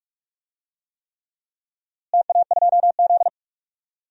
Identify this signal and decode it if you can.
Morse code, heard as a keyed tone, about 46 wpm, 700 Hz. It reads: TA2Z
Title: DXCC-100 Call Signs - 4 Characters - Repeated 3x - 45wpm